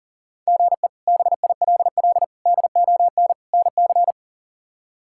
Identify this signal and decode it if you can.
Morse code, heard as a keyed tone, about 37 wpm, 700 Hz. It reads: GE BILL DON NC
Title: K1USN SST QSO Practice - 40wpm